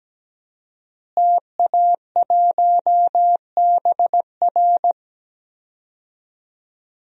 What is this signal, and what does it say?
Morse code, heard as a keyed tone, about 17 wpm, 700 Hz. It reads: TA1BR
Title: DXCC-100 Call Signs - 5 Characters - Repeated 2x - 17wpm